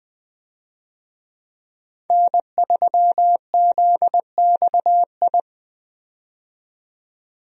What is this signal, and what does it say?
Morse code, heard as a keyed tone, about 21 wpm, 700 Hz. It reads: N3ZXI
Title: USA Call Signs - 5 Characters - 20wpm